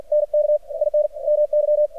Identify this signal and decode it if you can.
Morse code, heard as a keyed tone, about 53 wpm, 600 Hz. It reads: NK IN MY
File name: archived / CRSnet